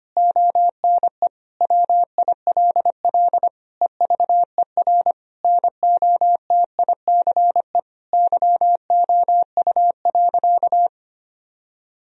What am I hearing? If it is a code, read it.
Morse code, heard as a keyed tone, about 25 wpm, 700 Hz. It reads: ONE WILL EVER NOTICE YOU.
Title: Sentences from Top 1000 Words - Review Part 4 - 25wpm